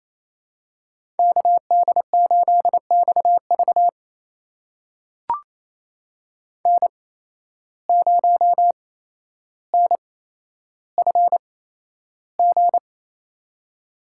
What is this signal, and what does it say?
Morse code, heard as a keyed tone, about 28 wpm, spaced as for 12 wpm, 700 Hz. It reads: KD8XV N0NFG
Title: USA Call Signs - 5 Characters - 8x Character Spacing - 28wpm